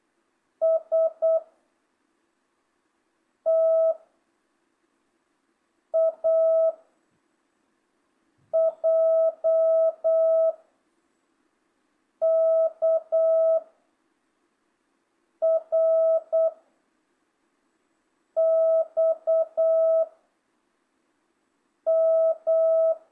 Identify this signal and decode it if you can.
Morse code, heard as a keyed tone, about 8 wpm, 660 Hz. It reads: S T A J K R X M
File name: jh8cbh / 2025cw